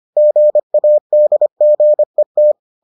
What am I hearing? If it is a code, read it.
Morse code, heard as a keyed tone, about 25 wpm, 600 Hz. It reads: GADGET